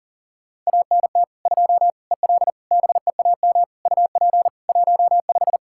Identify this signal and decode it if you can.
Morse code, heard as a keyed tone, about 37 wpm, 700 Hz. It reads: ANT 2 EL BEAM UP 15
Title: Practice QSO - Exchange 2 - Info - 40wpm